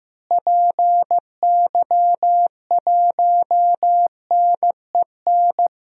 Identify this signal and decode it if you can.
Morse code, heard as a keyed tone, about 15 wpm, 700 Hz. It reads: PY1NEN
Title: DXCC-100 Call Signs - 6 Characters - Repeated 3x - 15wpm